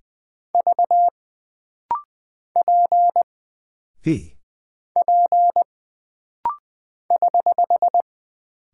Morse code, heard as a keyed tone, about 20 wpm, 700 Hz. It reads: V P P <HH>
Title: Single Character (Letter:Number:Punctuation:Prosign) - Rapid-Fire - 20wpm